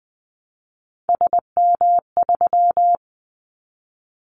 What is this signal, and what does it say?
Morse code, heard as a keyed tone, about 20 wpm, 700 Hz. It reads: SM3
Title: Callsign Prefixes for Top 25 DXCC - Repeated 3x - 20wpm